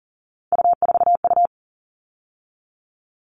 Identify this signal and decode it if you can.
Morse code, heard as a keyed tone, about 40 wpm, 700 Hz. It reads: U4U
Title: All Cut Numbers - 3-Digit 40wpm